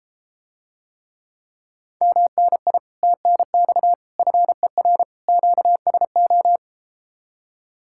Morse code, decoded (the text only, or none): MNI TNX FER QSO